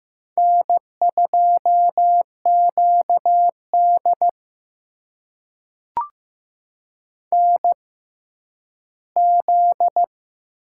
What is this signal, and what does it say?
Morse code, heard as a keyed tone, about 15 wpm, spaced as for 10 wpm, 700 Hz. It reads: N2QD NZ
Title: USA Call Signs - 4 Characters - 6x Character Spacing - 15wpm